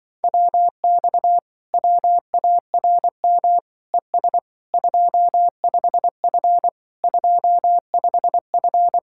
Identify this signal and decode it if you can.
Morse code, heard as a keyed tone, about 24 wpm, 700 Hz. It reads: WX WARM ES 25F 25F
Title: Speed-Racing Common QSO Phrases - 20wpm